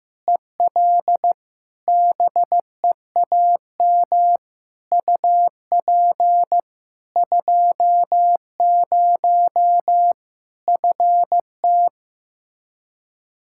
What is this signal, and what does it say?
Morse code, heard as a keyed tone, about 15 wpm, 700 Hz. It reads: EL BEAM UP 20 FT
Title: Practice QSO - Exchange 2 - Info - 15wpm